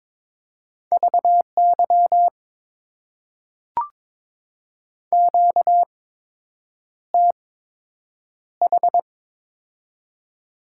Morse code, heard as a keyed tone, about 22 wpm, spaced as for 8 wpm, 700 Hz. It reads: VY QTH
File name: Top 90 QSO Elements - 8x Character Spacing - 22wpm